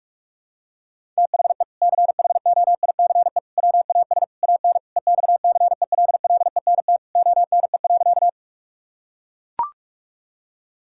Morse code, heard as a keyed tone, about 41 wpm, 700 Hz. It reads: THE CHOICE WAS AN EXCELLENT ONE.
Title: Sentences from Top 1000 Words - Review Part 6 - 45wpm